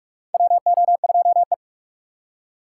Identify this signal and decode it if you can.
Morse code, heard as a keyed tone, about 44 wpm, 700 Hz. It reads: WO2E
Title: K1USN SST QSO Practice - 45wpm